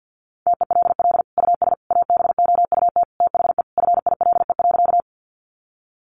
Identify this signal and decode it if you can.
Morse code, heard as a keyed tone, about 50 wpm, 700 Hz. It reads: TELL US ABOUT THE FILE.
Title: Sentences from Top 1200 Words 50wpm